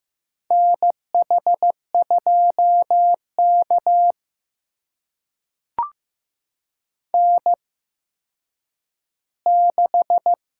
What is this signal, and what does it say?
Morse code, heard as a keyed tone, about 15 wpm, spaced as for 9 wpm, 700 Hz. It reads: NH2K N6
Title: USA Call Signs - 4 Characters - 8x Character Spacing - 15wpm